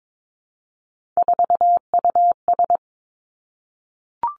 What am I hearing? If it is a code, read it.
Morse code, heard as a keyed tone, about 22 wpm, 700 Hz. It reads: 4US
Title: Three Challenging Letter-Number Combinations 22wpm